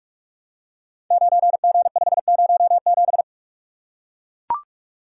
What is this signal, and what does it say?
Morse code, heard as a keyed tone, about 44 wpm, 700 Hz. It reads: 9G507
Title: Code Groups - 5 Letter:Numbers - 45wpm